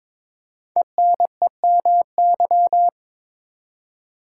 Morse code, heard as a keyed tone, about 22 wpm, 700 Hz. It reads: ENEMY
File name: 5 Letter Words - First 3 Letters Given - 22wpm